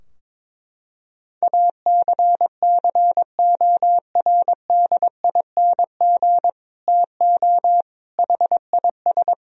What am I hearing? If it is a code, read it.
Morse code, heard as a keyed tone, about 22 wpm, 700 Hz. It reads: ACCORDING TO HIS
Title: Sentences from Top 1000 Words - Review Part 4 - 22wpm